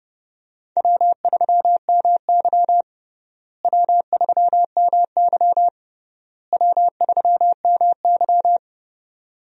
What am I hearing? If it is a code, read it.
Morse code, heard as a keyed tone, about 32 wpm, 700 Hz. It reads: W3MY W3MY W3MY
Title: USA Call Signs - 4 Characters - Repeated 3x - 30wpm